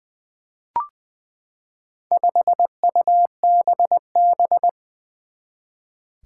Morse code, heard as a keyed tone, about 20 wpm, 700 Hz. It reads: 5UBB
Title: Four Challenging Letter-Number Combinations 20wpm